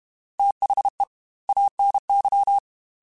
Morse code, heard as a keyed tone, about 32 wpm, 800 Hz. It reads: THE ANY
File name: Technical Subjects / Technical Subjects-mp3s / OR Gate @30